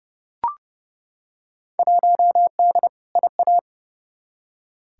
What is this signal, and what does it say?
Morse code, heard as a keyed tone, about 32 wpm, 700 Hz. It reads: 1D IA
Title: Field Day Exchanges - 30wpm